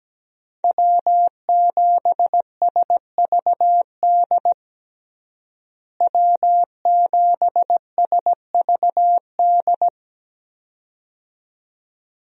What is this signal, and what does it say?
Morse code, heard as a keyed tone, about 17 wpm, 700 Hz. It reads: W7SVD W7SVD
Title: USA Call Signs - 5 Characters - Repeated 2x - 17wpm